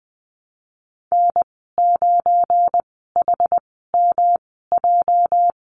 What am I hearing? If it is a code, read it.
Morse code, heard as a keyed tone, about 20 wpm, spaced as for 15 wpm, 700 Hz. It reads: N9HMJ
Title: 207 Lesson 33g - Letter j - Call Signs 20wpm